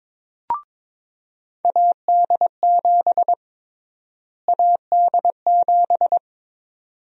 Morse code, heard as a keyed tone, about 22 wpm, 700 Hz. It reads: AD7 AD7
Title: USA Call Sign Prefixes - Repeated 2x - 22wpm